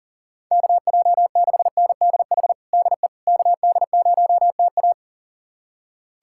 Morse code, heard as a keyed tone, about 40 wpm, 700 Hz. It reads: KJ6NDH DE KD0TA
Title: USA Callsign DE Callsign 40wpm